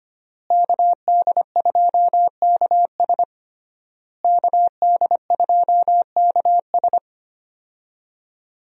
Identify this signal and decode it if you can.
Morse code, heard as a keyed tone, about 25 wpm, 700 Hz. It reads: KD2KS KD2KS
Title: USA Call Signs - 5 Characters - Repeated 3x - 25wpm